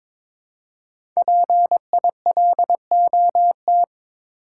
Morse code, heard as a keyed tone, about 22 wpm, 700 Hz. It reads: PILOT
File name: 5 Letter Words - First 3 Letters Given - 22wpm